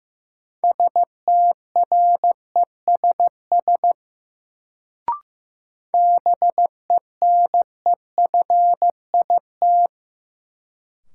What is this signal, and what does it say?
Morse code, heard as a keyed tone, about 15 wpm, 700 Hz. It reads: STRESS BENEFIT
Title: Sentences from Top 1000 Words - Review Part 6 - New Word Drill - 15wpm